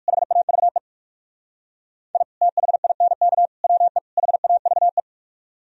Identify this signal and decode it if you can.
Morse code, heard as a keyed tone, about 58 wpm, 700 Hz. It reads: HAVE I THINK WE HAVE